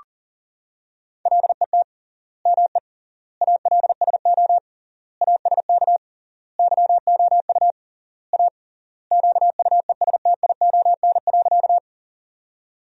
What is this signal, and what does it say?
Morse code, heard as a keyed tone, about 40 wpm, 700 Hz. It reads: LET ME ALSO ASK YOU A QUESTION.
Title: 3x Word-Spacing - Sentences from Top 400 Words - 40wpm